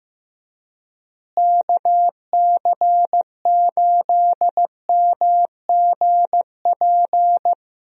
Morse code, heard as a keyed tone, about 15 wpm, 700 Hz. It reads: KC8MGP
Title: USA Call Signs - 6 Characters - Repeated 2x - 15wpm